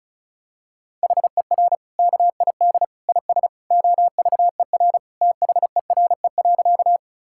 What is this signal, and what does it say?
Morse code, heard as a keyed tone, about 36 wpm, 700 Hz. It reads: HER KID IS OVER THERE.
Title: Sentences from Top 1600 Words 35wpm